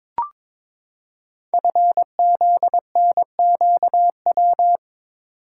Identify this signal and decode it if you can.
Morse code, heard as a keyed tone, about 22 wpm, 700 Hz. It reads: FZNQW
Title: Code Groups - 5 Letters - 22wpm